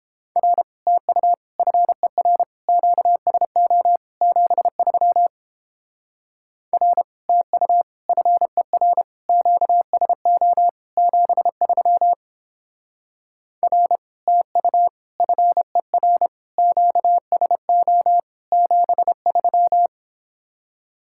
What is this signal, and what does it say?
Morse code, heard as a keyed tone, about 29 wpm, 700 Hz. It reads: R TU FER QSO 73 R TU FER QSO 73 R TU FER QSO 73